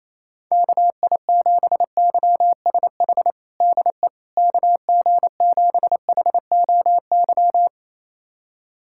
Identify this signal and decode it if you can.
Morse code, heard as a keyed tone, about 28 wpm, 700 Hz. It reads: KI7YSH DE KG7HOY
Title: USA Callsign DE Callsign 28wpm